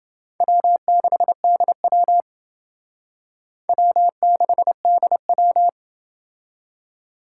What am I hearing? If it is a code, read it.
Morse code, heard as a keyed tone, about 28 wpm, 700 Hz. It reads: W6DW W6DW